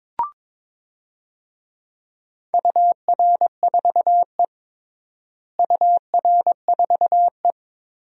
Morse code, heard as a keyed tone, about 22 wpm, 700 Hz. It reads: UR4E UR4E
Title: DXCC-100 Call Signs - 4 Characters - Repeated 2x - 22wpm